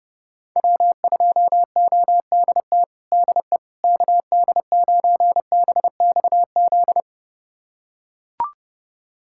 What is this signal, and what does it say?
Morse code, heard as a keyed tone, about 30 wpm, 700 Hz. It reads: W2ODT DE KD9BXZ
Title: USA Callsign DE Callsign 30wpm